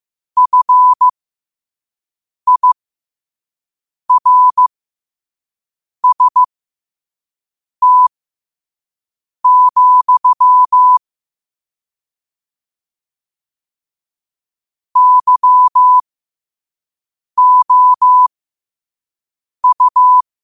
Morse code, heard as a keyed tone, about 15 wpm, spaced as for 5 wpm, 1000 Hz. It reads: FIRST, YOU
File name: qotdfiles / 5wpm